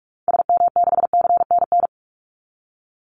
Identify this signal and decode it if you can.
Morse code, heard as a keyed tone, about 44 wpm, 700 Hz. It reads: SM6CNN
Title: Call Signs Repeated 2x - 45wpm